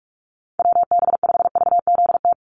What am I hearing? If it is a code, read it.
Morse code, heard as a keyed tone, about 46 wpm, 690 Hz. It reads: WB5VZT
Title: USA Call Signs 45wpm